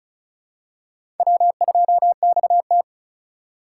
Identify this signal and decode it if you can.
Morse code, heard as a keyed tone, about 36 wpm, 700 Hz. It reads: W2XT
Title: USA Call Signs - 4 Characters - 3x Character Spacing - 35wpm